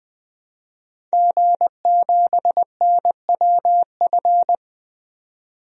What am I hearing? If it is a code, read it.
Morse code, heard as a keyed tone, about 20 wpm, 700 Hz. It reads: G7NWF
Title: DXCC-100 Call Signs - 5 Characters - 20wpm